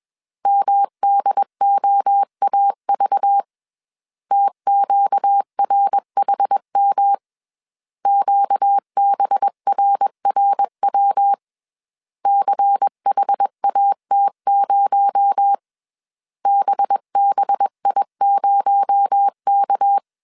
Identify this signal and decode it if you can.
Morse code, heard as a keyed tone, about 21 wpm, 800 Hz. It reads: MDOAV TQRHM QBRRW CHAT0 BBI0K
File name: CW / DET / 1 6 1500C S B 20wpm 2048